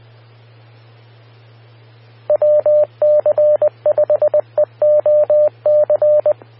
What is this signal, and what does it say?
Morse code, heard as a keyed tone, about 20 wpm, 600 Hz. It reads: WC5EOC
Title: Austin, TX Amateur Repeaters